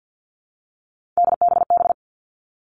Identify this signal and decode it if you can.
Morse code, heard as a keyed tone, about 48 wpm, 700 Hz. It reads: DBB
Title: All Cut Numbers - 3-Digit 50wpm